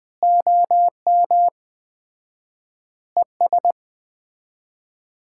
Morse code, heard as a keyed tone, about 20 wpm, 700 Hz. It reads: OM ES
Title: Common QSO Phrases - 4x Word-Spacing - 20wpm